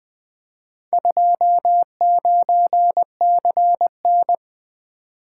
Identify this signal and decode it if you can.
Morse code, heard as a keyed tone, about 20 wpm, 700 Hz. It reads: 29CN
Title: Four Letter-Number Combinations - 20wpm